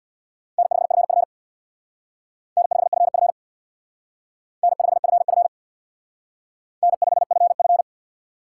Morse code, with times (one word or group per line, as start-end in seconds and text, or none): 0.6-1.2 N5FF
2.6-3.3 N5FF
4.6-5.5 N5FF
6.8-7.8 N5FF